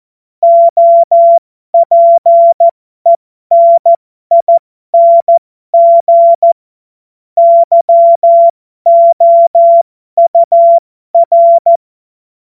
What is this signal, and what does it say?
Morse code, heard as a keyed tone, about 14 wpm, spaced as for 12 wpm, 680 Hz. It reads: OPENING YOUR